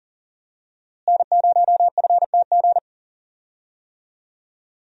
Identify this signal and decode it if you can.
Morse code, heard as a keyed tone, about 40 wpm, 700 Hz. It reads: N0FTG
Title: USA Call Signs - 5 Characters - Repeated 2x - 40wpm